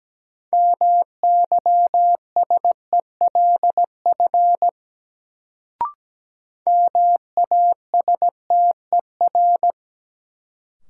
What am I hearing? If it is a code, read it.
Morse code, heard as a keyed tone, about 17 wpm, 700 Hz. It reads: MYSELF MASTER
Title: Sentences from Top 1300 Words - New Word Drill - 17wpm